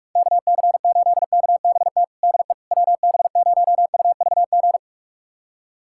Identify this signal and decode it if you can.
Morse code, heard as a keyed tone, about 43 wpm, 700 Hz. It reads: KC8KBT DE WB0UVG
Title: USA Callsign DE Callsign - 6 Characters - 45wpm